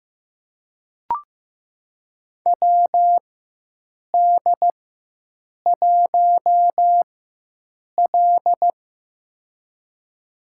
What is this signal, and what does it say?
Morse code, heard as a keyed tone, about 16 wpm, spaced as for 7 wpm, 700 Hz. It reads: WD1L
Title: USA Call Signs - 4 Characters - 4x Character Spacing - 15wpm